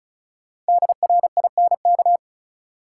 Morse code, heard as a keyed tone, about 36 wpm, 700 Hz. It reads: DRINK
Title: Sentences from Top 1500 Words - New Word Drill - 35wpm